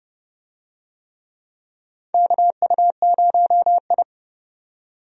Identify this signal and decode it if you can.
Morse code, heard as a keyed tone, about 30 wpm, 700 Hz. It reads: KU0I
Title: USA Call Signs - 4 Characters - 30wpm